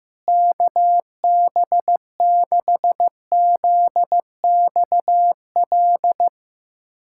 Morse code, heard as a keyed tone, about 16 wpm, 700 Hz. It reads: KB6ZXL
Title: USA Callsign DE Callsign - 6 Characters - 15wpm